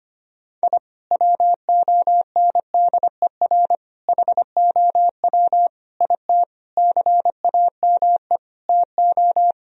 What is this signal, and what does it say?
Morse code, heard as a keyed tone, about 25 wpm, 700 Hz. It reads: I WONDER HOW IT CAME TO
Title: Sentences from Top 1000 Words - Review Part 3 - 25wpm